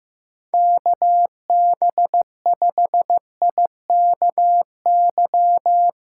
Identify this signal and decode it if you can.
Morse code, heard as a keyed tone, about 15 wpm, 700 Hz. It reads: KB5IKY
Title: USA Callsign DE Callsign - 6 Characters - 15wpm